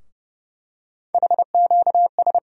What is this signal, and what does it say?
Morse code, heard as a keyed tone, about 31 wpm, 700 Hz. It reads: HQS
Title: USA Call Sign Postfixes - 3 Letters 30wpm